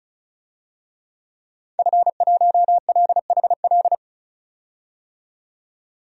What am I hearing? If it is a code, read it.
Morse code, heard as a keyed tone, about 35 wpm, 700 Hz. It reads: F1LHL
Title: DXCC-100 Call Signs - 5 Characters - 35wpm